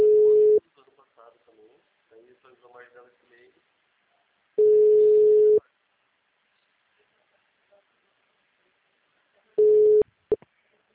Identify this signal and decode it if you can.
Ringback tone (Europe, 425 Hz), with speech.